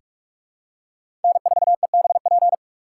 Morse code, heard as a keyed tone, about 46 wpm, 700 Hz. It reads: N4EBP
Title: USA Call Signs - 6x Character Spacing - 45wpm